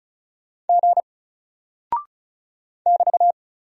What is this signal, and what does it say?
Morse code, heard as a keyed tone, about 34 wpm, 700 Hz. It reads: G <BT>
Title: Single Character (Letter:Number:Punctuation:Prosign) 35wpm